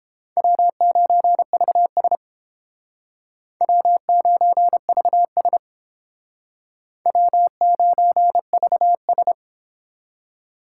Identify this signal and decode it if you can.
Morse code, heard as a keyed tone, about 30 wpm, 700 Hz. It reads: W9VS W9VS W9VS